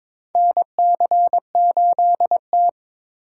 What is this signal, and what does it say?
Morse code, heard as a keyed tone, about 22 wpm, 700 Hz. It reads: NC8T
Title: USA Call Signs - 4 Characters - 8x Character Spacing - 22wpm